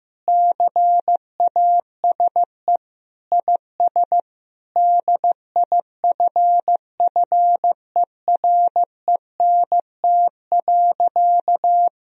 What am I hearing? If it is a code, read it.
Morse code, heard as a keyed tone, about 16 wpm, 700 Hz. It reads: CASE IS DIFFERENT.
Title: Sentences from Top 1000 Words - Review Part 3 - 15wpm